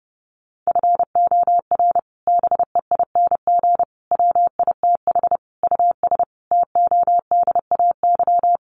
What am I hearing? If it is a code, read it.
Morse code, heard as a keyed tone, about 30 wpm, 700 Hz. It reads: FOR BEING WITH US TODAY
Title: Sets of 5 Words Encore2 30wpm